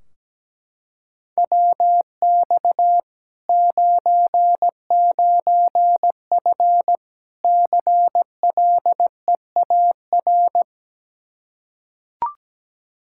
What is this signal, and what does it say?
Morse code, heard as a keyed tone, about 17 wpm, 700 Hz. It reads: WX 99F CLEAR
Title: Practice QSO - Exchange 2 - Info - 17wpm